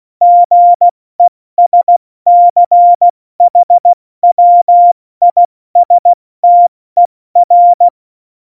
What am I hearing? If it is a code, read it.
Morse code, heard as a keyed tone, about 16 wpm, 700 Hz. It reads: GESCHWISTER
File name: Morsekurs / Witze / Witz 28. August 2018, 70 F80